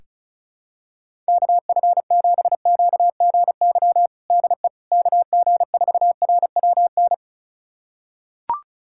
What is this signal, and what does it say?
Morse code, heard as a keyed tone, about 36 wpm, 700 Hz. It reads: KF7QGY DE KG4RWN
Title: USA Callsign DE Callsign - 6 Characters - 35wpm